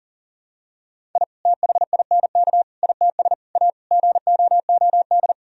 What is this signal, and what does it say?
Morse code, heard as a keyed tone, about 40 wpm, 700 Hz. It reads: I THINK ITS A GOOD
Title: Sentences from Top 1000 Words - Review Part 1 - 40wpm